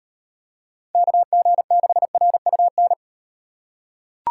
Morse code, heard as a keyed tone, about 38 wpm, 700 Hz. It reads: KG6RUN